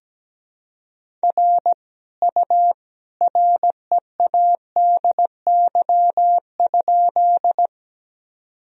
Morse code, heard as a keyed tone, about 17 wpm, 700 Hz. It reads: R U READY?
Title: Sentences from Top 400 Words and Abbreviations 17wpm